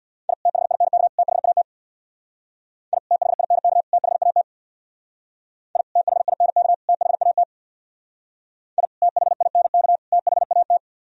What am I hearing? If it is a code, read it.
Morse code, heard as a keyed tone, about 62 wpm, 700 Hz. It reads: I THINK THAT I THINK THAT I THINK THAT I THINK THAT
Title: Speed-Racing - Sets of 3 Words 50wpm